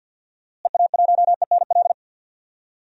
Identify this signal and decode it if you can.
Morse code, heard as a keyed tone, about 52 wpm, 700 Hz. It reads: EA1ENL
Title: DXCC-100 Call Signs - 6 Characters - Repeated 3x - 50wpm